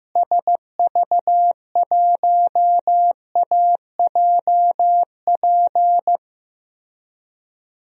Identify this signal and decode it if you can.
Morse code, heard as a keyed tone, about 15 wpm, 700 Hz. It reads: SV1AJP
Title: DXCC-100 Call Signs - 6 Characters - Repeated 3x - 15wpm